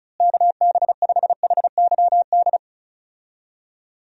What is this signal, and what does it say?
Morse code, heard as a keyed tone, about 34 wpm, 700 Hz. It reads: KB5HYD
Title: USA Call Signs - 6 Characters - Repeated 2x - 35wpm